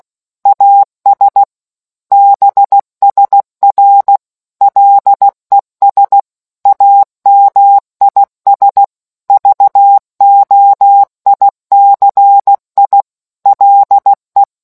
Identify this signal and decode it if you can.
Morse code, heard as a keyed tone, about 16 wpm, 800 Hz. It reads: AS BSR LES AMIS VOICI LE